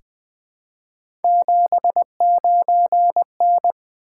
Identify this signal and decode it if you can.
Morse code, heard as a keyed tone, about 20 wpm, 700 Hz. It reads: 79N
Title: Three Letter-Number Combinations - 20wpm